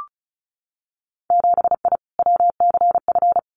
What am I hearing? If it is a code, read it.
Morse code, heard as a keyed tone, about 34 wpm, 700 Hz. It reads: 7I WCF
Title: Winter Field Day Exchanges Encore 35wpm